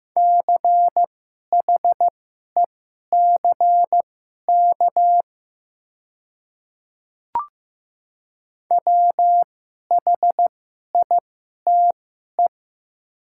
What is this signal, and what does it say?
Morse code, heard as a keyed tone, about 15 wpm, spaced as for 11 wpm, 700 Hz. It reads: CHECK WHITE
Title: 2x Character Spacing - Top 500 Words - 15wpm